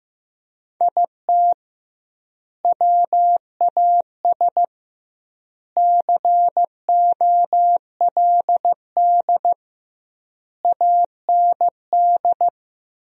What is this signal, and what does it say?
Morse code, heard as a keyed tone, about 16 wpm, 700 Hz. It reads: IT WAS COLD AND
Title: 2x Word-Spacing - Sentences from Top 500 Words - 15wpm